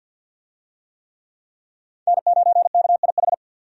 Morse code, heard as a keyed tone, about 51 wpm, 700 Hz. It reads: N9KIH
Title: USA Call Signs - 5 Characters - 50wpm